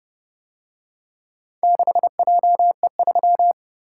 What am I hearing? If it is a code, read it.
Morse code, heard as a keyed tone, about 30 wpm, 700 Hz. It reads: 6JE3